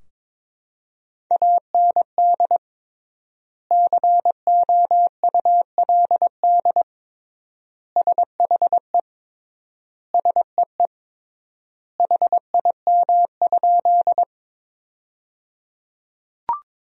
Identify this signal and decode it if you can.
Morse code, heard as a keyed tone, about 22 wpm, 700 Hz. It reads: AND COULD SHE SEE HIM?
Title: 3x Word-Spacing - Sentences from Top 100 Words - 22wpm